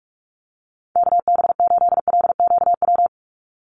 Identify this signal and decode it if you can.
Morse code, heard as a keyed tone, about 44 wpm, 700 Hz. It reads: KB8LQW
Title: USA Call Signs 45wpm